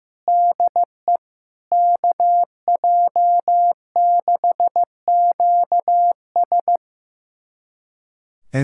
Morse code, heard as a keyed tone, about 15 wpm, 700 Hz. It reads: DE KJ6QS
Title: USA Callsign DE Callsign - 5 Characters - 15wpm